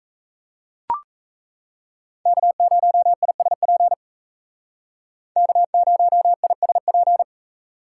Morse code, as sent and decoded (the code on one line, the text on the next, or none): -.- ----- .. ... .--. / -.- ----- .. ... .--.
K0ISP K0ISP